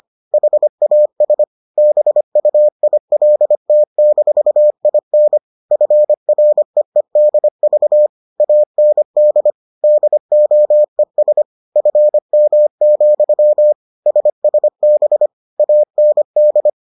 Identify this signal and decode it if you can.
Morse code, heard as a keyed tone, about 25 wpm, 600 Hz. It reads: HAS BUILT-IN FREEDV AND DOES FM, SSB AND